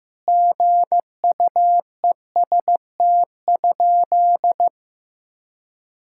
Morse code, heard as a keyed tone, about 15 wpm, 700 Hz. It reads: GUEST?
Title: Sentences from Top 1600 Words 15wpm